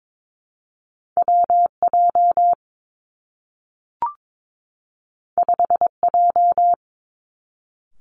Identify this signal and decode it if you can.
Morse code, heard as a keyed tone, about 22 wpm, 700 Hz. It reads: W J 5 J
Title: Two Challenging Letter-Number Combinations 22wpm